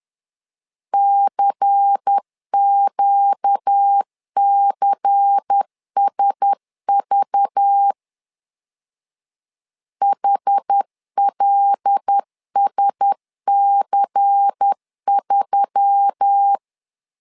Morse code, heard as a keyed tone, about 10 wpm, 800 Hz. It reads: CQCSV HLSC3
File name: LEZIONI / 5 LBVC / 5 LBVC 10wpm 2349